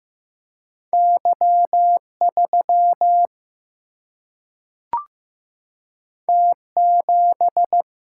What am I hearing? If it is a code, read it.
Morse code, heard as a keyed tone, about 15 wpm, 700 Hz. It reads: Y3 T7
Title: Single-Letter Single-Number 15wpm